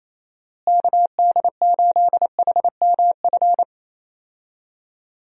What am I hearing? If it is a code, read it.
Morse code, heard as a keyed tone, about 28 wpm, 700 Hz. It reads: KD8HMF
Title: USA Call Signs - 6 Characters - Repeated 2x - 28wpm